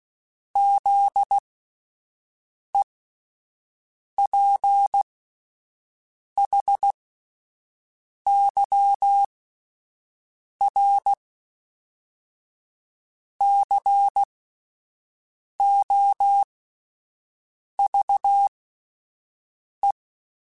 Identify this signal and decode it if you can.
Morse code, heard as a keyed tone, about 16 wpm, spaced as for 5 wpm, 800 Hz. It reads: ZEPHYR COVE